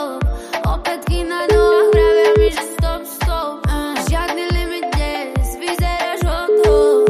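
Ringback tone (Europe, 425 Hz), with noise.
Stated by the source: Ringback tone from O2 Czech Republic, heard by the caller before the call is answered.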